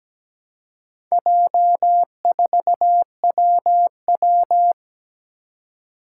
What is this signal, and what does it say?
Morse code, heard as a keyed tone, about 17 wpm, 700 Hz. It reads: J4WW
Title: Four Challenging Letter-Number Combinations 17wpm